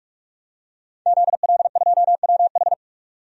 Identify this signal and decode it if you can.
Morse code, heard as a keyed tone, about 46 wpm, 700 Hz. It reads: ZL2WH
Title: DXCC-100 Call Signs - 5 Characters - 45wpm